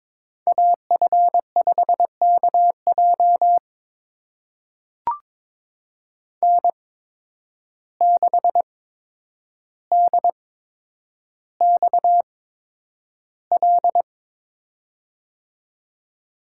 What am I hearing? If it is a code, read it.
Morse code, heard as a keyed tone, about 22 wpm, spaced as for 9 wpm, 700 Hz. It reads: AF5KJ N6DXL
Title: USA Call Signs - 5 Characters - 8x Character Spacing - 22wpm